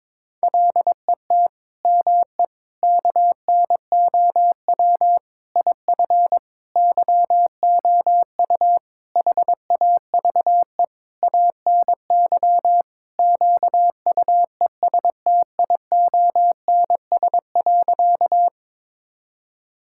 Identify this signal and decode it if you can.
Morse code, heard as a keyed tone, about 22 wpm, 700 Hz. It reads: LET ME KNOW IF YOU HAVE ANY QUESTIONS.
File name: Sentences with 7-Word Phrases - 22wpm